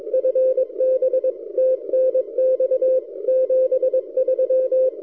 Morse code, heard as a keyed tone, about 22 wpm, 520 Hz. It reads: FB TNX 73
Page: during this recording I switched the rx loop direction front to the backside K3ZM almost disappeared so my K9AY loop seems to work okay recorded amazing 160m signal